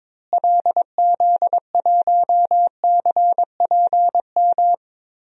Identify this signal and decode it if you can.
Morse code, heard as a keyed tone, about 22 wpm, 700 Hz. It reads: LZ1CPM
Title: DXCC-100 Call Signs - 6 Characters - Repeated 3x - 22wpm